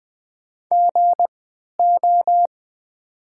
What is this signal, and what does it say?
Morse code, heard as a keyed tone, about 20 wpm, spaced as for 11 wpm, 700 Hz. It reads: GO
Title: 3x Character Spacing - Top 300 Words - 20wpm